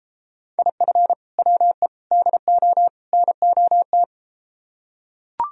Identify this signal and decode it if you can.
Morse code, heard as a keyed tone, about 33 wpm, 700 Hz. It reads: IF WE DO NOT